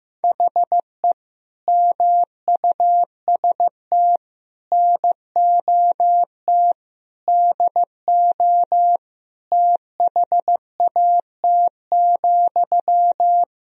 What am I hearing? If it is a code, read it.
Morse code, heard as a keyed tone, about 15 wpm, 700 Hz. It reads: HE MUST NOT DO THAT,
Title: Sentences from Top 1000 Words - Review Part 2 - 15wpm